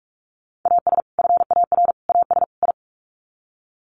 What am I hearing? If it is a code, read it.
Morse code, heard as a keyed tone, about 44 wpm, 700 Hz. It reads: AS FAR AS I
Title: Sets of 4 Words 45wpm